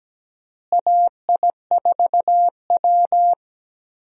Morse code, heard as a keyed tone, about 17 wpm, 700 Hz. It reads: AI4W
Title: USA Call Signs - 4 Characters - Repeated 2x - 17wpm